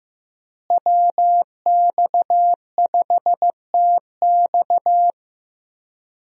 Morse code, heard as a keyed tone, about 15 wpm, 700 Hz. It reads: WX5TX
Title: USA Call Signs - 5 Characters - Repeated 3x - 15wpm